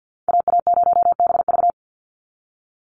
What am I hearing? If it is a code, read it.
Morse code, heard as a keyed tone, about 50 wpm, 700 Hz. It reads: AA0BV
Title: NA SOTA Activator Callsigns 50wpm